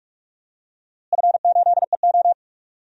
Morse code, heard as a keyed tone, about 42 wpm, 700 Hz. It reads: F8EO
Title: DXCC-100 Call Signs - 4 Characters - Repeated 3x - 45wpm